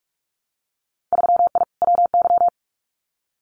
Morse code, heard as a keyed tone, about 46 wpm, 700 Hz. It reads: 3I WY
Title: Winter Field Day Exchanges Encore 45wpm